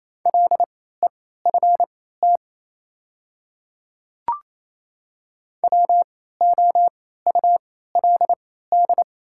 Morse code, heard as a keyed tone, about 28 wpm, spaced as for 16 wpm, 700 Hz. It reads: LEFT WOULD
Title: 3x Character Spacing - Top 300 Words - 28wpm